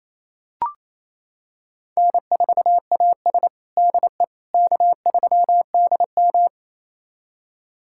Morse code, heard as a keyed tone, about 28 wpm, 700 Hz. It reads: N4AS DE K3DM
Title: USA Callsign DE Callsign - 4 Characters - 28wpm